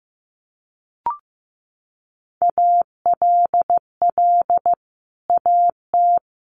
Morse code, heard as a keyed tone, about 15 wpm, 700 Hz. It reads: ALL AT
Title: Sets of 5 Words Encore2 15wpm